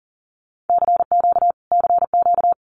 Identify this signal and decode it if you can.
Morse code, heard as a keyed tone, about 40 wpm, 700 Hz. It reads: CQ CQ
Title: Common QSO Phrases Encore 40wpm